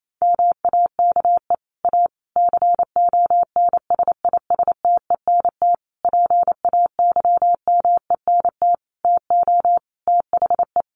Morse code, heard as a keyed tone, about 28 wpm, 700 Hz. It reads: MAKE A CONSISTENT PAYMENT TO THE
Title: Sentences from Top 1500 Words 28wpm